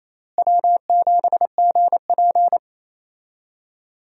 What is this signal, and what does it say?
Morse code, heard as a keyed tone, about 27 wpm, 700 Hz. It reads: W7GP
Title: USA Call Signs - 4 Characters - Repeated 3x - 28wpm